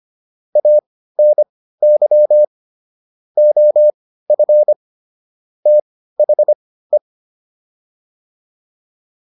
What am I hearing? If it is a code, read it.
Morse code, heard as a keyed tone, about 25 wpm, spaced as for 15 wpm, 600 Hz. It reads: ANY OF THE
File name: WD204 15